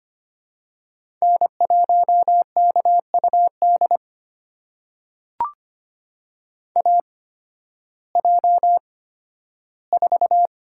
Morse code, heard as a keyed tone, about 25 wpm, spaced as for 13 wpm, 700 Hz. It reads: N1KUD AJ4
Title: USA Call Signs - 5 Characters - 8x Character Spacing - 25wpm